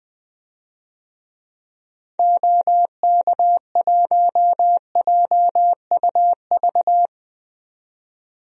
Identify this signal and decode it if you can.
Morse code, heard as a keyed tone, about 20 wpm, 700 Hz. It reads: OK1JUV
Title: DXCC-100 Call Signs - 6 Characters - 20wpm